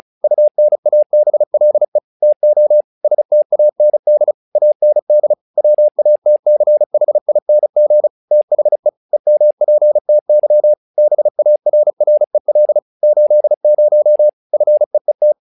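Morse code, heard as a keyed tone, about 34 wpm, 600 Hz. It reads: UNABLE TO STAND AND WATCHING THE EMPTY BARREL 80 FEET
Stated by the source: Morse practice text "Tower Trouble" sent at 35 WPM, from the SKCC - Straight Key Century Club learning center audio files.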